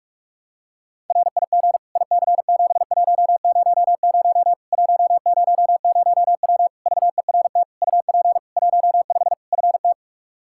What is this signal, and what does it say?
Morse code, heard as a keyed tone, about 48 wpm, 700 Hz. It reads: RIG IC7100 100W VERT UP 15 FT
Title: K1USN SST QSO Practice 45wpm